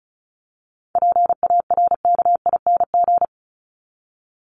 Morse code, heard as a keyed tone, about 34 wpm, 700 Hz. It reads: PARKING
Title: Top 1501-1750 Words 35wpm